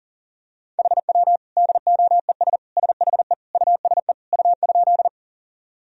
Morse code, heard as a keyed tone, about 40 wpm, 700 Hz. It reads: HW DOES SHE USE U?
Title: Sentences from Top 200 Words and Abbreviations 40wpm